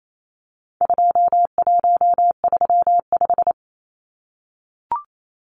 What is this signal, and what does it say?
Morse code, heard as a keyed tone, about 28 wpm, 700 Hz. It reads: 2135
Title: Numbers - 4-Digit 28wpm